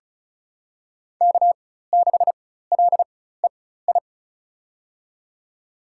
Morse code, heard as a keyed tone, about 36 wpm, spaced as for 16 wpm, 700 Hz. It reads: K6LEI
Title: USA Call Signs - 5 Characters - 4x Character Spacing - 35wpm